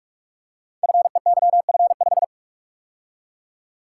Morse code, heard as a keyed tone, about 44 wpm, 700 Hz. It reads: FEYF5
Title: Code Groups - 5 Letter:Numbers - 45wpm